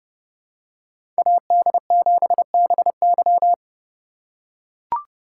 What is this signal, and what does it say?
Morse code, heard as a keyed tone, about 30 wpm, 700 Hz. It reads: AD7BY
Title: USA Call Signs - 2x Character Spacing - 30wpm